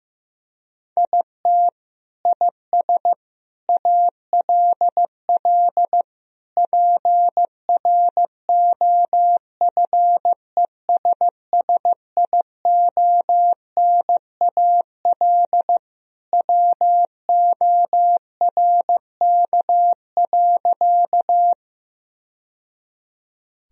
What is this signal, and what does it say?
Morse code, heard as a keyed tone, about 15 wpm, 700 Hz. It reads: IT IS ALL PROFESSIONAL WORK.
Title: Sentences from Top 1000 Words - Review Part 4 - 15wpm